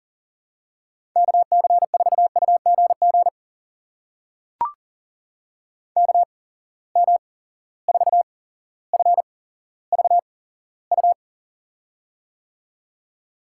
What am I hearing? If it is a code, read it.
Morse code, heard as a keyed tone, about 40 wpm, spaced as for 17 wpm, 700 Hz. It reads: KC4UGG KM4FVU
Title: USA Call Signs - 6 Characters - 8x Character Spacing - 40wpm